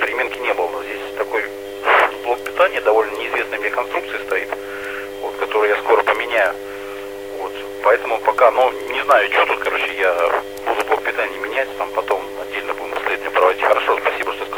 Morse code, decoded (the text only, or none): none